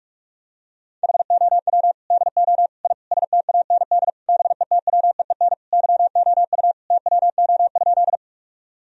Morse code, heard as a keyed tone, about 46 wpm, 700 Hz. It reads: HOW DO I STAND BETWEEN YOU TWO?
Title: Sentences from Top 1000 Words - Review Part 1 - 45wpm